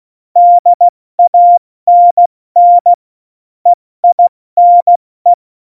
Morse code, heard as a keyed tone, about 16 wpm, 700 Hz. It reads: DANN EINE
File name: Morsekurs / Witze / Witz 08. Januar 2019, 70 F80